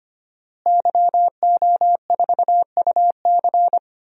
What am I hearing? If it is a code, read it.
Morse code, heard as a keyed tone, about 25 wpm, 700 Hz. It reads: YO4UC
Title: DXCC-100 Call Signs - 5 Characters - Repeated 2x - 25wpm